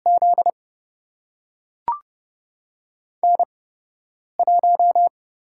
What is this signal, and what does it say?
Morse code, heard as a keyed tone, about 29 wpm, spaced as for 8 wpm, 700 Hz. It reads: Z N1
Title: USA Call Signs - 5 Characters - 8x Character Spacing - 30wpm